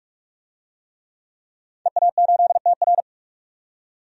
Morse code, heard as a keyed tone, about 44 wpm, 700 Hz. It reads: EA8TR
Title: DXCC-100 Call Signs - 5 Characters - 45wpm